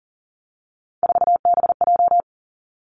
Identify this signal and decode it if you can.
Morse code, heard as a keyed tone, about 41 wpm, 690 Hz. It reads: 4BJ
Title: Three Challenging Letter-Number Combinations 40wpm